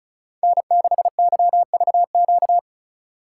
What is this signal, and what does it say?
Morse code, heard as a keyed tone, about 36 wpm, 700 Hz. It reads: N6YVQ
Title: USA Call Signs - 5 Characters - 4x Character Spacing - 35wpm